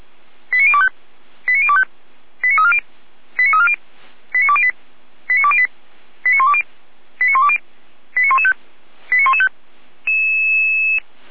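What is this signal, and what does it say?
Five-tone selective call (ZVEI), heard as quick runs of five tones, tones of about 70 ms. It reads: 89025 89025 89340 89340 89208 89208 89120 89120 89115 89115
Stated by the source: ZVEI five-tone selective call 89092 received by SDR.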